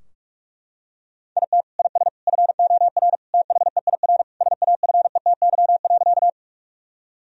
Morse code, heard as a keyed tone, about 42 wpm, 700 Hz. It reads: IT IS FOR THEIR SAFETY.
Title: Sentences from Top 1000 Words - Review Part 4 - 45wpm